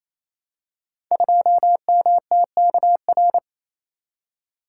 Morse code, heard as a keyed tone, about 27 wpm, 690 Hz. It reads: 2MTKR
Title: Code Groups - 5 Letter:Numbers - 28wpm